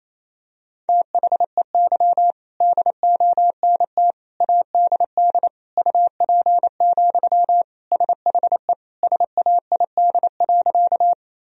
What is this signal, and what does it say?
Morse code, heard as a keyed tone, about 28 wpm, 700 Hz. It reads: THEY DONT ADD UP, SHE SAID.
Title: Sentences from Top 1000 Words - Review Part 1 - 28wpm